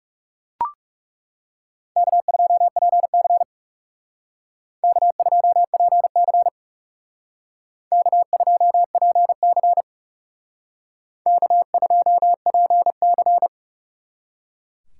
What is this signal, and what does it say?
Morse code, heard as a keyed tone, about 36 wpm, 700 Hz. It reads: K2PC K2PC K2PC K2PC
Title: Speed Racing - USA 1x2 Call Signs 30wpm